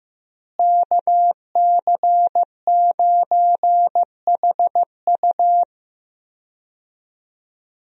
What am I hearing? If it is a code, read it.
Morse code, heard as a keyed tone, about 15 wpm, 700 Hz. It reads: KC9HU
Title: USA Call Signs - 5 Characters - Repeated 3x - 15wpm